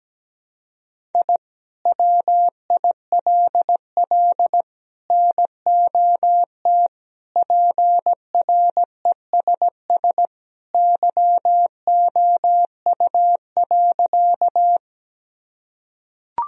Morse code, heard as a keyed tone, about 17 wpm, 700 Hz. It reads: I WILL NOT PRESS YOU.
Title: Sentences from Top 1000 Words - Review Part 1 - 17wpm